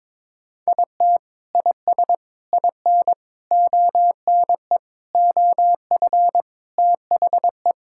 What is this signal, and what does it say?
Morse code, heard as a keyed tone, about 22 wpm, 700 Hz. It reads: IT IS IN ONE OF THE
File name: practice / 22wpm / Sentences with 3-Word Phrases - 22wpm